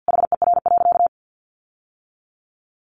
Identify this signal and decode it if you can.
Morse code, heard as a keyed tone, about 50 wpm, 700 Hz. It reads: HER.
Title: Sentences from Top 1200 Words 50wpm